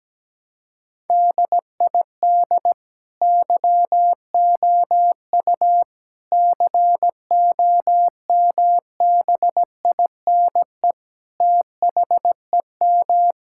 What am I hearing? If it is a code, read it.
Morse code, heard as a keyed tone, about 17 wpm, 700 Hz. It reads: DID YOU COMBINE THEM
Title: Sentences from Top 1600 Words 17wpm